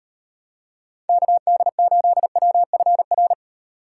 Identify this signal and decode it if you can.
Morse code, heard as a keyed tone, about 30 wpm, 700 Hz. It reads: KD8WFR
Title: Speed Racing - USA 2x3 Call Signs 25wpm